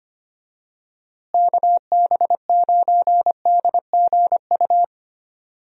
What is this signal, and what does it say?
Morse code, heard as a keyed tone, about 25 wpm, 700 Hz. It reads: KB9DGU
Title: USA Call Signs - 6 Characters - Repeated 3x - 25wpm